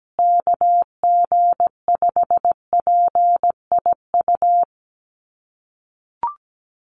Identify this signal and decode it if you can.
Morse code, heard as a keyed tone, about 17 wpm, 700 Hz. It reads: KG5PIU
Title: USA Call Signs Repeated 3x 17wpm